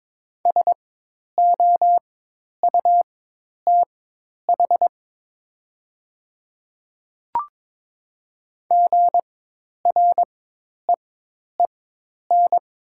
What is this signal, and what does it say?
Morse code, heard as a keyed tone, about 22 wpm, spaced as for 10 wpm, 700 Hz. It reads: SOUTH GREEN
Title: 4x Character Spacing - Top 500 Words - 22wpm